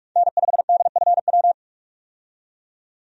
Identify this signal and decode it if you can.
Morse code, heard as a keyed tone, about 46 wpm, 700 Hz. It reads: N5DFW
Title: USA Call Signs - 5 Characters - 45wpm